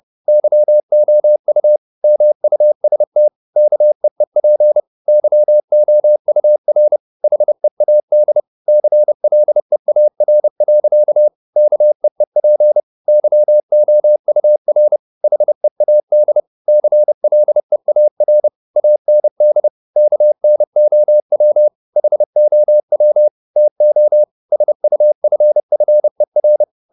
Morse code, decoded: YOU MUST KEEP YOUR HEAD CLEAR. KEEP YOUR HEAD CLEAR AND KNOW HOW TO SUFFER

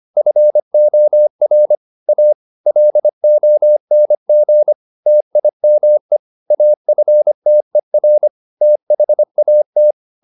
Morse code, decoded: FOR A LONG TIME AFTER THAT

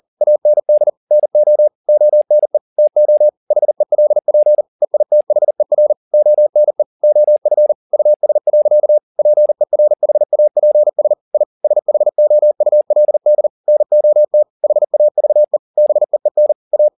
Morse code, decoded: AND NO ONE TO HELP EITHER ONE OF US. PERHAPS I SHOULD NOT HAVE BEEN A

